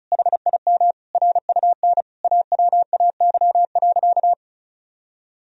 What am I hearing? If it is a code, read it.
Morse code, heard as a keyed tone, about 34 wpm, 700 Hz. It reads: HIM RUN AWAY.
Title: Sentences from Top 1000 Words - Review Part 1 - 35wpm